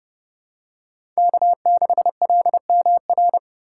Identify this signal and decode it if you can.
Morse code, heard as a keyed tone, about 29 wpm, 700 Hz. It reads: K6LMR